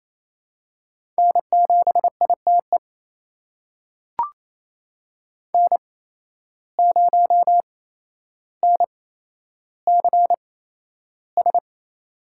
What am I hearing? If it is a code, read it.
Morse code, heard as a keyed tone, about 28 wpm, spaced as for 12 wpm, 700 Hz. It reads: N7ITE N0NCS
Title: USA Call Signs - 5 Characters - 8x Character Spacing - 28wpm